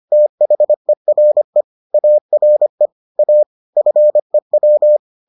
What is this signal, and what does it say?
Morse code, heard as a keyed tone, about 25 wpm, 600 Hz. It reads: THERE ARE A FEW